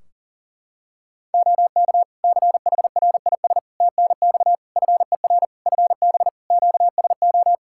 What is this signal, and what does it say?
Morse code, heard as a keyed tone, about 40 wpm, 700 Hz. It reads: OK CHRIS TNX FER FB QSO
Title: Practice QSO - Exchange 4 - Ending - 40wpm